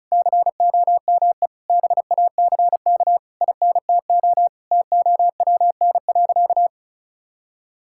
Morse code, heard as a keyed tone, about 36 wpm, 700 Hz. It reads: COME BACK INTO TOWN.
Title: Sentences from Top 1000 Words - Review Part 6 - 35wpm